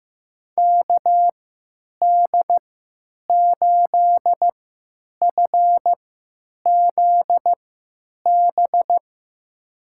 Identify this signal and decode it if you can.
Morse code, heard as a keyed tone, about 15 wpm, spaced as for 9 wpm, 700 Hz. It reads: KD8FZB
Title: USA Call Signs - 6 Characters - 3x Character Spacing - 15wpm